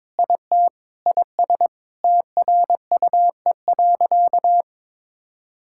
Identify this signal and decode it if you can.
Morse code, heard as a keyed tone, about 22 wpm, 700 Hz. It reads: IT IS TRUE.
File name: practice / 22wpm / Sentences with 4-Word Phrases - 22wpm